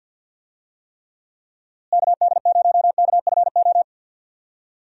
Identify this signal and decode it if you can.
Morse code, heard as a keyed tone, about 50 wpm, 700 Hz. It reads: KD0KFO
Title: USA Call Signs - 6 Characters - Repeated 3x - 50wpm